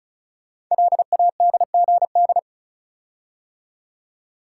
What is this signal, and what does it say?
Morse code, heard as a keyed tone, about 36 wpm, 700 Hz. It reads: LADGD
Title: Code Groups - 5 Letters - 35wpm